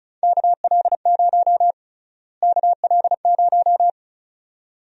Morse code, heard as a keyed tone, about 36 wpm, 700 Hz. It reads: KL0 KL0
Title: USA Call Sign Prefixes - Repeated 2x - 35wpm